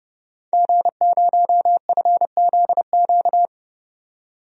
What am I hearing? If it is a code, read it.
Morse code, heard as a keyed tone, about 30 wpm, 690 Hz. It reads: G0FZQ
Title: DXCC-100 Call Signs - 5 Characters - 30wpm